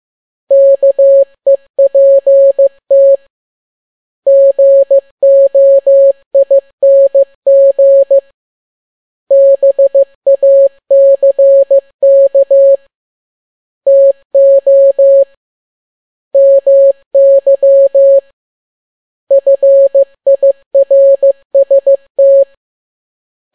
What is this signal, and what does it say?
Morse code, heard as a keyed tone, about 15 wpm, 550 Hz. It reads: KEPT GOING BACK TO MY FIRST